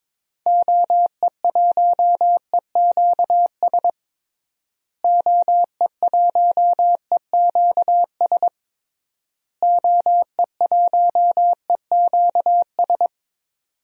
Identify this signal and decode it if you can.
Morse code, heard as a keyed tone, about 22 wpm, 700 Hz. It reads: OE1EQS OE1EQS OE1EQS
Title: DXCC-100 Call Signs - 6 Characters - Repeated 3x - 22wpm